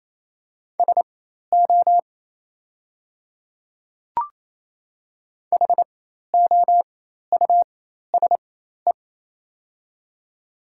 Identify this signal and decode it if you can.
Morse code, heard as a keyed tone, about 28 wpm, spaced as for 13 wpm, 690 Hz. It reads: SO HOUSE
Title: 4x Character Spacing - Top 200 Words - 28wpm